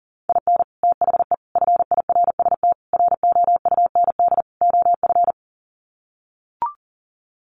Morse code, heard as a keyed tone, about 40 wpm, 700 Hz. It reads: IN THE FIRST ROUND OF
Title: Sets of 5 Words Encore 40wpm